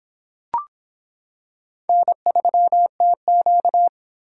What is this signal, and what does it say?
Morse code, heard as a keyed tone, about 26 wpm, 700 Hz. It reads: N3TQ